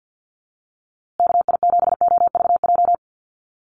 Morse code, heard as a keyed tone, about 48 wpm, 700 Hz. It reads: KI7OVJ
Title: USA Call Signs Repeated 3x 50wpm